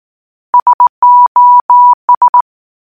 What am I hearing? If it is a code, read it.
Morse code, heard as a keyed tone, about 18 wpm, 980 Hz. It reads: SOS
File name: Sos Morse Code Ringtone